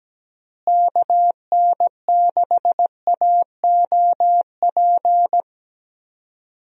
Morse code, heard as a keyed tone, about 17 wpm, 700 Hz. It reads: KN6AOP
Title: USA Call Signs - 6 Characters - Repeated 2x - 17wpm